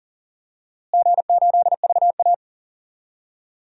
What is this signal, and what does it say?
Morse code, heard as a keyed tone, about 39 wpm, 700 Hz. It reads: G8VA
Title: DXCC-100 Call Signs - 4 Characters - Repeated 2x - 40wpm